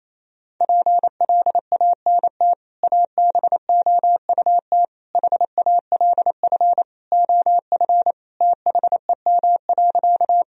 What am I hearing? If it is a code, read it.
Morse code, heard as a keyed tone, about 28 wpm, 700 Hz. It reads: PLANT ABOUT HALF OF THEM.
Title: Sentences from Top 1000 Words - Review Part 1 - 28wpm